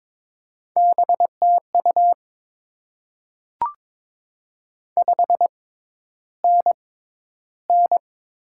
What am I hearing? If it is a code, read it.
Morse code, heard as a keyed tone, about 22 wpm, spaced as for 11 wpm, 700 Hz. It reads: BTU 5NN
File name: Top 20 QSO Elements - 6x Character Spacing - 22wpm